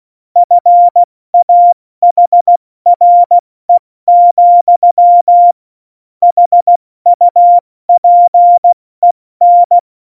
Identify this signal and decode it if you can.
Morse code, heard as a keyed tone, about 16 wpm, 700 Hz. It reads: FAHRE, HUPEN
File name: Morsekurs / Witze / Witz 05. Februar 2019, 70 F80